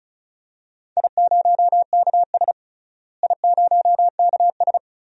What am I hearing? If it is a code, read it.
Morse code, heard as a keyed tone, about 36 wpm, 700 Hz. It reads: I0KS I0KS
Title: DXCC-100 Call Signs - 4 Characters - Repeated 2x - 35wpm